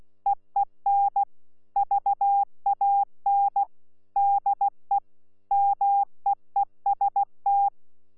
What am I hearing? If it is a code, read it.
Morse code, heard as a keyed tone, about 16 wpm, 800 Hz. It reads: EEN VAN DE MEEST
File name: Z PI4VRZ / PI4VRZ 14-jun-2025